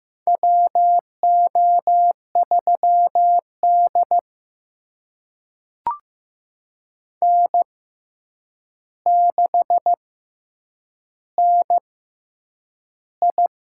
Morse code, heard as a keyed tone, about 15 wpm, spaced as for 8 wpm, 700 Hz. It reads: WO3D N6NI
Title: USA Call Signs - 6x Character Spacing - 15wpm